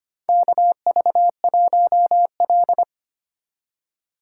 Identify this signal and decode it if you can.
Morse code, heard as a keyed tone, about 25 wpm, 690 Hz. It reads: KV1L
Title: USA Call Signs - 4 Characters - 4x Character Spacing - 25wpm